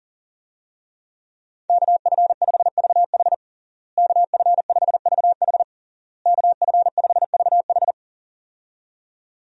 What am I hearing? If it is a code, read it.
Morse code, heard as a keyed tone, about 40 wpm, 700 Hz. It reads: KF5VH KF5VH KF5VH
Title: USA Call Signs - 5 Characters - Repeated 3x - 40wpm